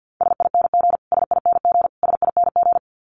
Morse code, heard as a keyed tone, about 50 wpm, 700 Hz. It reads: SING SING SING
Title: Top 500 Words Repeated 50wpm